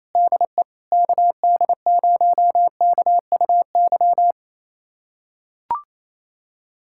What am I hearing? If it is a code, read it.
Morse code, heard as a keyed tone, about 27 wpm, 700 Hz. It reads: DE KD0KUY
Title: USA Callsign DE Callsign 28wpm